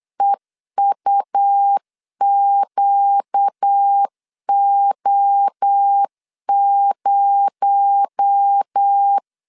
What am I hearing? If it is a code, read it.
Morse code, heard as a keyed tone, about 8 wpm, 800 Hz. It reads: EUQO0
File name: 8wpm